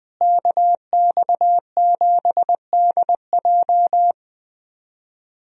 Morse code, heard as a keyed tone, about 20 wpm, 700 Hz. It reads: KX7DJ
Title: USA Call Signs - 5 Characters - Repeated 3x - 20wpm